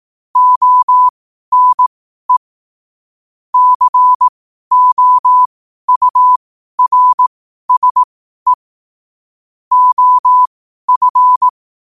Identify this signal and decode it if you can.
Morse code, heard as a keyed tone, about 18 wpm, spaced as for 12 wpm, 1000 Hz. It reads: ONE COURSE OF
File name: qotdfiles / 12wpm